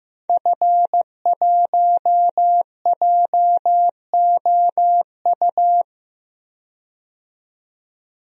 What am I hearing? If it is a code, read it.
Morse code, heard as a keyed tone, about 15 wpm, 700 Hz. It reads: F1JOU
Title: DXCC-100 Call Signs - 5 Characters - 15wpm